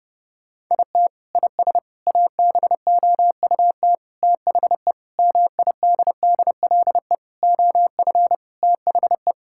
Morse code, heard as a keyed tone, about 30 wpm, 700 Hz. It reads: IT IS ABOUT THE MIDDLE OF THE
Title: Sentences from Top 1000 Words - Review Part 6 - 30wpm